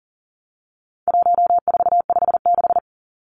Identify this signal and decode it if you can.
Morse code, heard as a keyed tone, about 40 wpm, 700 Hz. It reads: 1456
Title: Numbers - 4-Digit 40wpm